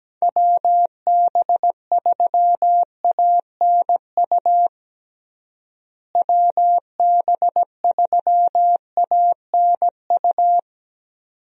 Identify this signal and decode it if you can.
Morse code, heard as a keyed tone, about 17 wpm, 700 Hz. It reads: WB3ANU WB3ANU
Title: USA Call Signs - 6 Characters - Repeated 2x - 17wpm